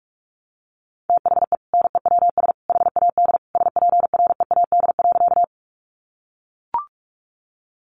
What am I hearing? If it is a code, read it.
Morse code, heard as a keyed tone, about 44 wpm, 700 Hz. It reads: THE NEWS HAD SPREAD.
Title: Sentences from Top 1200 Words 45wpm